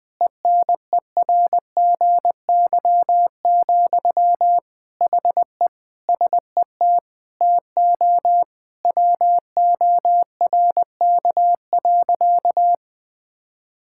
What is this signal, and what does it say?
Morse code, heard as a keyed tone, about 20 wpm, 700 Hz. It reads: ENERGY, HE SET TO WORK.
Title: Sentences from Top 1000 Words - Review Part 6 - 20wpm